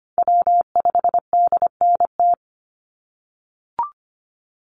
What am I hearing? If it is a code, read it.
Morse code, heard as a keyed tone, about 25 wpm, 700 Hz. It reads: W5DNT
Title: Call Signs - 25wpm